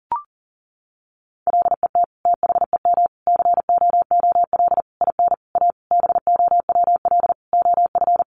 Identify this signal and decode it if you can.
Morse code, heard as a keyed tone, about 40 wpm, 700 Hz. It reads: LET THEM COOL IN A BOWL OF
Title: Sentences from Top 1200 Words 40wpm